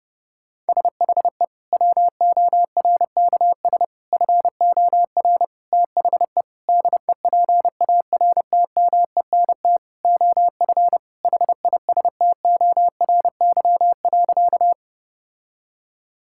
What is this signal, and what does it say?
Morse code, heard as a keyed tone, about 32 wpm, 700 Hz. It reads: SHE WORKS FOR THE DEPARTMENT OF HISTORY.
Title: Sentences from Top 1000 Words - Review Part 5 - 30wpm